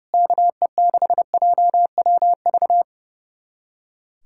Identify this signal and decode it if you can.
Morse code, heard as a keyed tone, about 30 wpm, 700 Hz. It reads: KE6JWV
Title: USA Callsign DE Callsign 30wpm